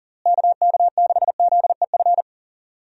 Morse code, heard as a keyed tone, about 40 wpm, 700 Hz. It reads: KK6ZEF
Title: USA Call Signs - 6 Characters - Repeated 3x - 40wpm